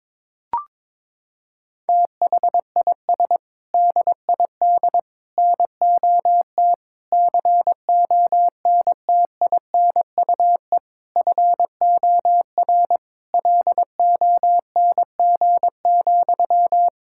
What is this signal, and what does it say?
Morse code, heard as a keyed tone, about 22 wpm, 700 Hz. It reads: THIS DID NOT CONTINUE FOR LONG,
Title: Sentences from Top 1000 Words - Review Part 6 - 22wpm